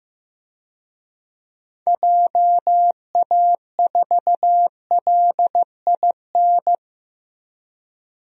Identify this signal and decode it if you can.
Morse code, heard as a keyed tone, about 15 wpm, 700 Hz. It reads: JA4LIN
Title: DXCC-100 Call Signs - 6 Characters - Repeated 2x - 15wpm